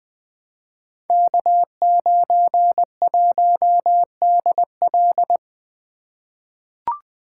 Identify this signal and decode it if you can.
Morse code, heard as a keyed tone, about 20 wpm, 700 Hz. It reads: K91DL
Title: Code Groups - 5 Letter:Numbers - 20wpm